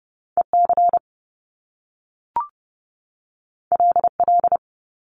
Morse code, heard as a keyed tone, about 30 wpm, 700 Hz. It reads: EC LL
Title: Top Two Letter Combinations in Words 30wpm